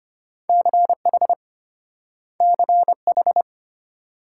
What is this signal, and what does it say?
Morse code, heard as a keyed tone, about 34 wpm, 700 Hz. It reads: CH CH